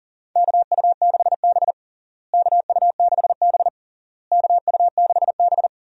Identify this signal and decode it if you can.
Morse code, heard as a keyed tone, about 40 wpm, 700 Hz. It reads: KU6B KU6B KU6B
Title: USA Call Signs - 4 Characters - Repeated 3x - 40wpm